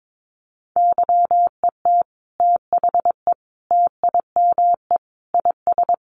Morse code, heard as a keyed tone, about 22 wpm, 700 Hz. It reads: YET THE TIME IS
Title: Sentences from Top 1100 Words 22wpm